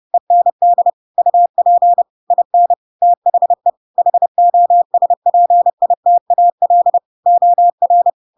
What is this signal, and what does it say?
Morse code, heard as a keyed tone, about 30 wpm, 700 Hz. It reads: END UP IN THE HOSPITAL OR
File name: Pinocchio / Pinocchio ch4 30 30 ik1hge